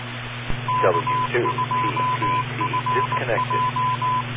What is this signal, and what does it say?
Morse code, heard as a keyed tone, about 19 wpm, 1010 Hz, with radio noise. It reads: K2FJ